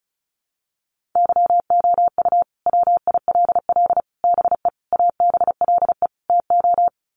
Morse code, heard as a keyed tone, about 34 wpm, 700 Hz. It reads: YOU WILL BE ABLE TO
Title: Sets of 5 Words Encore2 35wpm